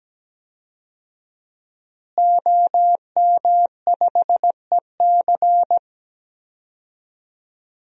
Morse code, heard as a keyed tone, about 17 wpm, 700 Hz. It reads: OM5EC
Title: DXCC-100 Call Signs - 5 Characters - Repeated 3x - 17wpm